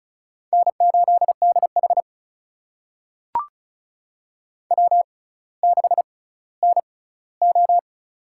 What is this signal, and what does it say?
Morse code, heard as a keyed tone, about 36 wpm, spaced as for 18 wpm, 700 Hz. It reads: N8DH W6NO
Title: USA Call Signs - 4 Characters - 6x Character Spacing - 35wpm